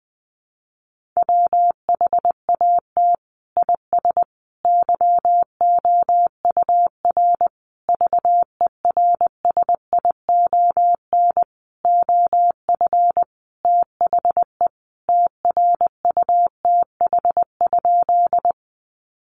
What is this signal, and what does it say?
Morse code, heard as a keyed tone, about 20 wpm, 700 Hz. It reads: WHAT IS YOUR VERSION OF THE TRUTH?
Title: Sentences from Top 1400 Words 20wpm